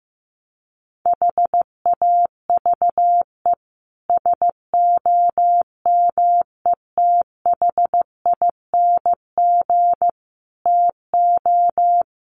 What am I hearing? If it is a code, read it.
Morse code, heard as a keyed tone, about 15 wpm, 700 Hz. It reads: HAVE SOMETHING TO
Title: Sets of 5 Words Encore 15wpm